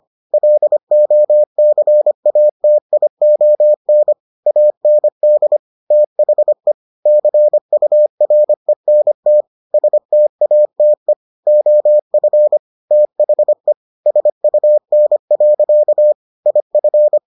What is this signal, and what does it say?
Morse code, heard as a keyed tone, about 25 wpm, 600 Hz. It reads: LOCATION AND THE CURENT STATE OF THE SUN. IF